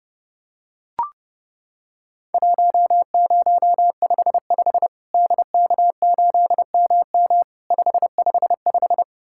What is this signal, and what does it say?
Morse code, heard as a keyed tone, about 30 wpm, 700 Hz. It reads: 1055 DK8MM 555
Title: Worked All Europe DX Contest - QTC Exchange - 30wpm